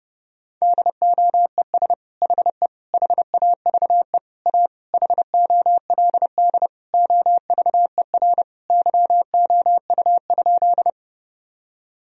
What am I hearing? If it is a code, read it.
Morse code, heard as a keyed tone, about 32 wpm, 700 Hz. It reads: DOES HE HAVE A HOLD OVER YOU?
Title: Sentences from Top 1000 Words - Review Part 2 - 30wpm